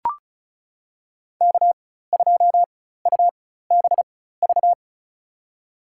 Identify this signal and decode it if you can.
Morse code, heard as a keyed tone, about 34 wpm, spaced as for 16 wpm, 700 Hz. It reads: K2UBV
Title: USA Call Signs - 5 Characters - 4x Character Spacing - 35wpm